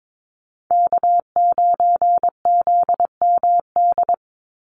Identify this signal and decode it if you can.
Morse code, heard as a keyed tone, about 22 wpm, 700 Hz. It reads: K9ZMD
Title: NA SOTA Chaser Callsigns 22wpm